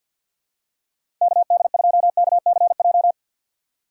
Morse code, heard as a keyed tone, about 46 wpm, 700 Hz. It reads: KD2KCJ
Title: USA Call Signs - 6 Characters - 6x Character Spacing - 50wpm